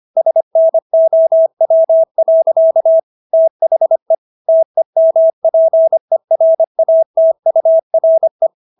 Morse code, heard as a keyed tone, about 25 wpm, 650 Hz. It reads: SNOW. THE TEMPERATURE